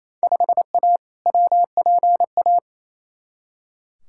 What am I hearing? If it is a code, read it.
Morse code, heard as a keyed tone, about 28 wpm, 700 Hz. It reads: 5A WPA
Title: Field Day Exchanges Encore - 28wpm